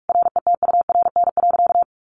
Morse code, heard as a keyed tone, about 44 wpm, 700 Hz. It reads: RETURN.
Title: Sentences from Top 1100 Words 45wpm